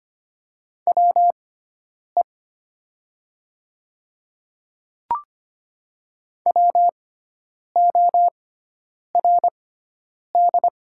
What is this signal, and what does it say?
Morse code, heard as a keyed tone, about 25 wpm, spaced as for 9 wpm, 700 Hz. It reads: WE WORD
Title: 6x Character Spacing - Top 100 Words - 25wpm